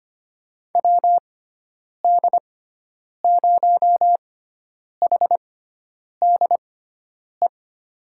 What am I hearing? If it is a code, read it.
Morse code, heard as a keyed tone, about 25 wpm, spaced as for 9 wpm, 700 Hz. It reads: WD0HDE
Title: USA Call Signs - 6 Characters - 6x Character Spacing - 25wpm